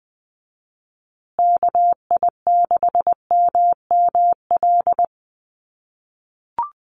Morse code, heard as a keyed tone, about 20 wpm, 700 Hz. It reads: KI6MML
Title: USA Call Signs Repeated 3x 20wpm